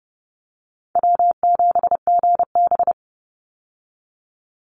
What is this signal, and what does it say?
Morse code, heard as a keyed tone, about 29 wpm, 700 Hz. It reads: W7GB
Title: Call Signs - 30wpm